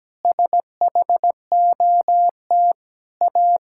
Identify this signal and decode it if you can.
Morse code, heard as a keyed tone, about 17 wpm, 700 Hz. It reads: SHOT A
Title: Pangram Challenge v2 - 17wpm